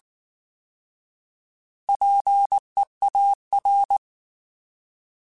Morse code, heard as a keyed tone, about 19 wpm, 800 Hz. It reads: PEAR
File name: Alphabetical / Alphabetical mp3s / Ps @18